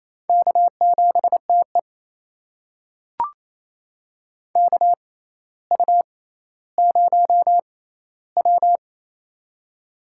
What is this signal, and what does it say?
Morse code, heard as a keyed tone, about 27 wpm, spaced as for 14 wpm, 700 Hz. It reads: K7TE KU0W
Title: USA Call Signs - 4 Characters - 6x Character Spacing - 28wpm